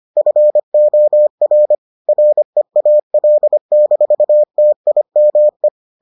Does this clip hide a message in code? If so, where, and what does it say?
Yes: Morse code from 0.2 to 5.7 s, heard as a keyed tone, about 25 wpm, 600 Hz. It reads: FOR REAL-TIME